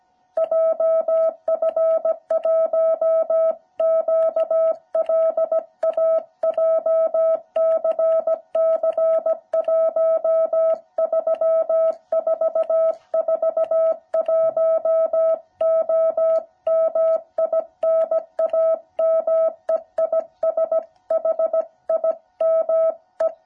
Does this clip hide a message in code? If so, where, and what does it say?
Yes: Morse code from 0.4 to 23.3 s, heard as a keyed tone, about 17 wpm, 650 Hz. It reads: JF1QLA JCC 13441O MI NAME IS HIME